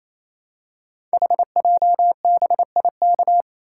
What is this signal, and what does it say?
Morse code, heard as a keyed tone, about 27 wpm, 700 Hz. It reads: HJBIK
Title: Code Groups - 5 Letters - 28wpm